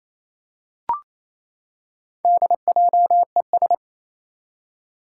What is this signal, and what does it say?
Morse code, heard as a keyed tone, about 28 wpm, 700 Hz. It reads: DJES
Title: Four Challenging Letter-Number Combinations 28wpm